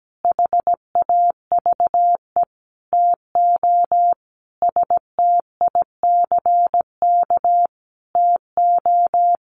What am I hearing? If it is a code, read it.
Morse code, heard as a keyed tone, about 17 wpm, 700 Hz. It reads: HAVE TO STICK TO
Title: Sentences from Top 1200 Words 17wpm